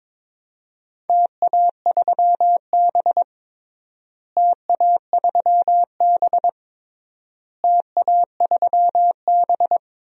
Morse code, heard as a keyed tone, about 21 wpm, 700 Hz. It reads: TA3B TA3B TA3B
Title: DXCC-100 Call Signs - 4 Characters - Repeated 3x - 22wpm